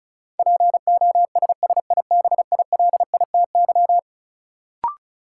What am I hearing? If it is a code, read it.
Morse code, heard as a keyed tone, about 36 wpm, 700 Hz. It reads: POSSIBILITY
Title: Sets of 3 Words - New Word Drill - 35wpm